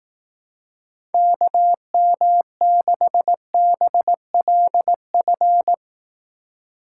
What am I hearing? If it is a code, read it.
Morse code, heard as a keyed tone, about 18 wpm, 700 Hz. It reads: KM6BLF
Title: Speed Racing - USA 2x3 Call Signs 15wpm